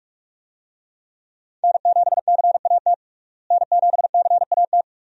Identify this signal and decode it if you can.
Morse code, heard as a keyed tone, about 46 wpm, 700 Hz. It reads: N7CAT N7CAT
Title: USA Call Signs - 5 Characters - Repeated 3x - 45wpm